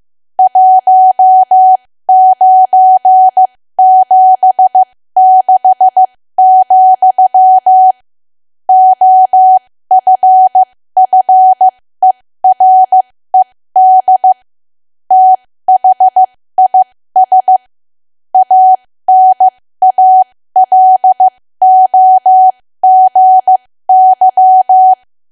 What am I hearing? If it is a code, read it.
Morse code, heard as a keyed tone, about 15 wpm, spaced as for 13 wpm, 750 Hz. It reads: 1976, OFFERED THIS ANALOGY
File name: cwa / W1AW-Level2 / 13WPM